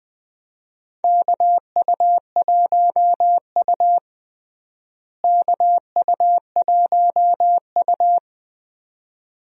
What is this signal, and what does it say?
Morse code, heard as a keyed tone, about 21 wpm, 700 Hz. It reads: KU1U KU1U
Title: USA Call Signs - 4 Characters - Repeated 2x - 20wpm